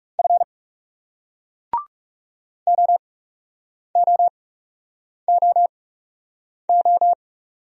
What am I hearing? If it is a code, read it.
Morse code, heard as a keyed tone, about 41 wpm, spaced as for 8 wpm, 700 Hz. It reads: F OOOO